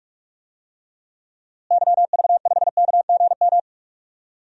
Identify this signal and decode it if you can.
Morse code, heard as a keyed tone, about 43 wpm, 700 Hz. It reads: YV5KGM
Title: DXCC-100 Call Signs - 6 Characters - Repeated 3x - 45wpm